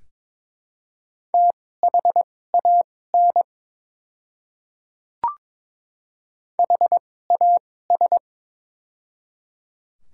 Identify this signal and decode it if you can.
Morse code, heard as a keyed tone, about 22 wpm, spaced as for 16 wpm, 700 Hz. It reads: THAN HAS
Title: 2x Character Spacing - Top 200 Words - 22wpm